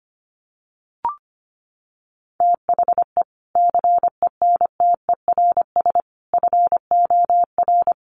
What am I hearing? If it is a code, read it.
Morse code, heard as a keyed tone, about 25 wpm, 700 Hz. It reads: THE CENTERS FOR
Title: Sets of 5 Words Encore2 25wpm